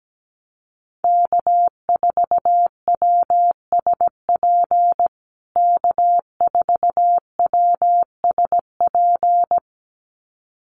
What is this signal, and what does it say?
Morse code, heard as a keyed tone, about 17 wpm, 700 Hz. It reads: K4WSP K4WSP
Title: USA Call Signs Repeated 2x 17wpm